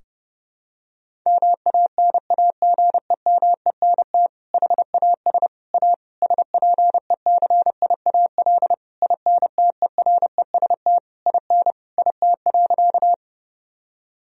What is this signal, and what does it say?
Morse code, heard as a keyed tone, about 30 wpm, 700 Hz. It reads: MANAGEMENT HAS A SPECIAL INTEREST IN IT.
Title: Sentences from Top 1000 Words - Review Part 4 - 30wpm